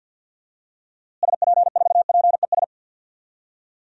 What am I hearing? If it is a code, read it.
Morse code, heard as a keyed tone, about 50 wpm, 700 Hz. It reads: SP4PES
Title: DXCC-100 Call Signs - 6 Characters - Repeated 2x - 50wpm